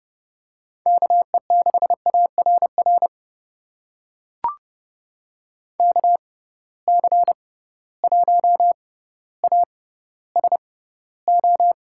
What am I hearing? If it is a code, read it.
Morse code, heard as a keyed tone, about 30 wpm, spaced as for 15 wpm, 700 Hz. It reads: KE6ARR KC1ASO
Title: USA Call Signs - 6 Characters - 6x Character Spacing - 30wpm